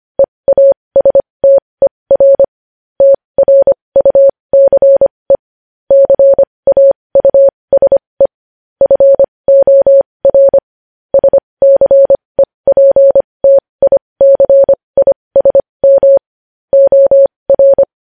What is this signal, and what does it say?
Morse code, heard as a keyed tone, about 25 wpm, spaced as for 20 wpm, 550 Hz. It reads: EASTER TRUCE CAUSE FOR SCEPTICISM OR
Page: news-headlines-in-morse-code-at-20-wpm